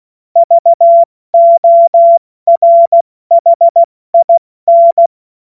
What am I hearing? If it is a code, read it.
Morse code, heard as a keyed tone, about 16 wpm, 680 Hz. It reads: VORHIN
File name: Morsekurs / Witze / Witz 11. Juli 2023 70 F80